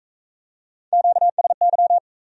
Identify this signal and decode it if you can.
Morse code, heard as a keyed tone, about 41 wpm, 700 Hz. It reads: QSY